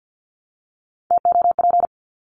Morse code, heard as a keyed tone, about 50 wpm, 700 Hz. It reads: TOP
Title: Top 500 Words 50wpm